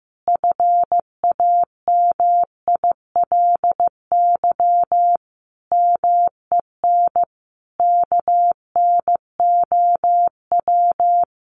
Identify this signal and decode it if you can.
Morse code, heard as a keyed tone, about 15 wpm, 700 Hz. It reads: FAMILY MEN KNOW
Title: Sentences from Top 1400 Words 15wpm